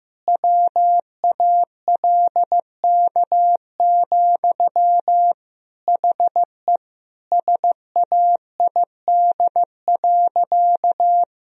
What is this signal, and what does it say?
Morse code, heard as a keyed tone, about 15 wpm, 700 Hz. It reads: WALK, HE SAID.
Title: Sentences from Top 1000 Words - Review Part 1 - 15wpm